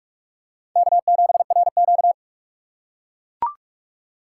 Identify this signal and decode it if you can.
Morse code, heard as a keyed tone, about 42 wpm, 700 Hz. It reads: K7RQ